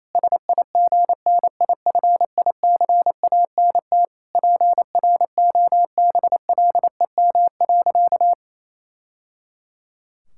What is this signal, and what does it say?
Morse code, heard as a keyed tone, about 28 wpm, 700 Hz. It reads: SIGNIFICANT PROBLEM.
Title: Sentences from Top 1000 Words - Review Part 6 - 28wpm